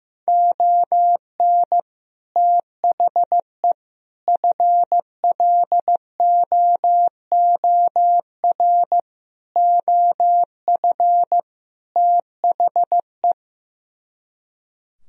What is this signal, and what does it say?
Morse code, heard as a keyed tone, about 15 wpm, 700 Hz. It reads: ON THE FLOOR OF THE
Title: Speed-Racing - Sets of 5 Words 15wpm